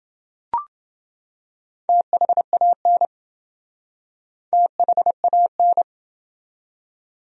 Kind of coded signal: Morse code

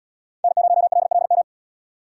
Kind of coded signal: Morse code

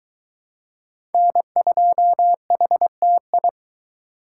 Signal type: Morse code

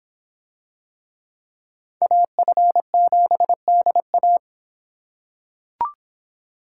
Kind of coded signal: Morse code

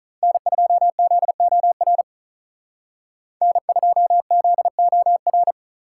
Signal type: Morse code

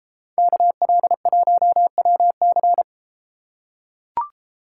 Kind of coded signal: Morse code

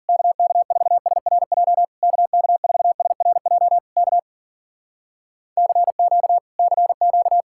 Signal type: Morse code

